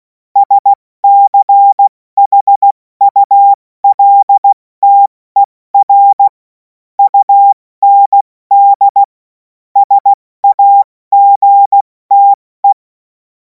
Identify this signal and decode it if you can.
Morse code, heard as a keyed tone, about 16 wpm, 800 Hz. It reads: SCHULTER UND SAGTE
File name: Morsekurs / Witze / Witz 31. März 2026 70 F80